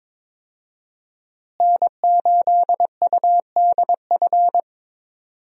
Morse code, heard as a keyed tone, about 22 wpm, 700 Hz. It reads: N8UDF
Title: USA Call Signs - 5 Characters - Repeated 2x - 22wpm